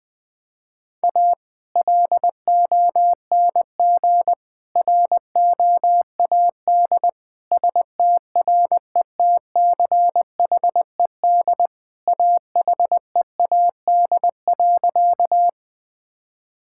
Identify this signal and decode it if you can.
Morse code, heard as a keyed tone, about 20 wpm, 700 Hz. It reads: A LONG ROAD STRETCHED AHEAD.
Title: Sentences with 2-Word Phrases - 20wpm